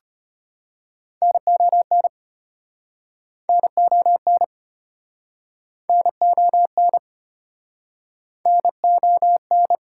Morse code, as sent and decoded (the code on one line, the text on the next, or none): -. --- -. / -. --- -. / -. --- -. / -. --- -.
NON NON NON NON